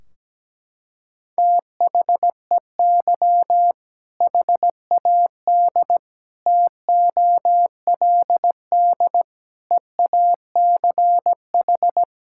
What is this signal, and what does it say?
Morse code, heard as a keyed tone, about 17 wpm, 700 Hz. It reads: THEY HAD TOLD EACH
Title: Sentences from Top 1600 Words 17wpm